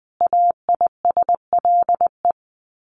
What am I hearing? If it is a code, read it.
Morse code, heard as a keyed tone, about 20 wpm, 700 Hz. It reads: AISLE
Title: Sets of 5 Words Encore2 20wpm